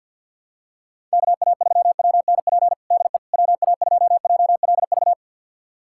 Morse code, heard as a keyed tone, about 48 wpm, 700 Hz. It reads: KA3WNP DE WA2JLV
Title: USA Callsign DE Callsign - 6 Characters - 50wpm